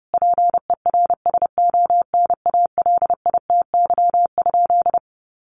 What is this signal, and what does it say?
Morse code, heard as a keyed tone, about 30 wpm, 700 Hz. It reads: PERSONALITY?
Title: Sentences from Top 1400 Words 30wpm